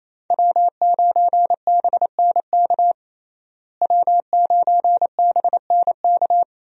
Morse code, heard as a keyed tone, about 28 wpm, 700 Hz. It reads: W9BNK W9BNK
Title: USA Call Signs - 5 Characters - Repeated 2x - 28wpm